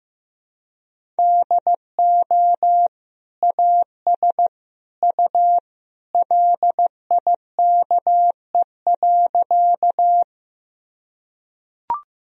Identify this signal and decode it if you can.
Morse code, heard as a keyed tone, about 15 wpm, 700 Hz. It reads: DO AS U LIKE.
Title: Sentences from Top 100 Words and Abbreviations 15wpm